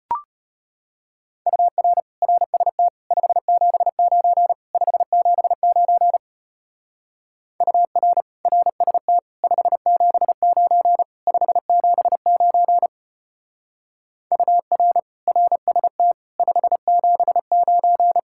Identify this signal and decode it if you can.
Morse code, heard as a keyed tone, about 31 wpm, 700 Hz. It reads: UR RST 579 579 UR RST 579 579 UR RST 579
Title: Speed-Racing Common QSO Phrases Encore - 25wpm